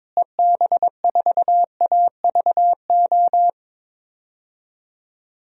Morse code, heard as a keyed tone, about 22 wpm, 700 Hz. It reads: EB4AVO
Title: DXCC-100 Call Signs - 6 Characters - Repeated 3x - 22wpm